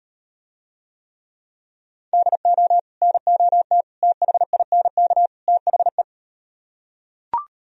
Morse code, heard as a keyed tone, about 36 wpm, 700 Hz. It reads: DO NOT THINK THE